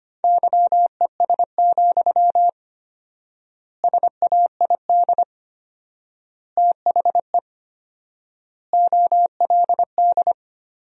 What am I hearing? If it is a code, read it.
Morse code, heard as a keyed tone, about 25 wpm, 700 Hz. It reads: YES, SAID THE OLD
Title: 4x Word-Spacing - Sentences from Top 500 Words - 25wpm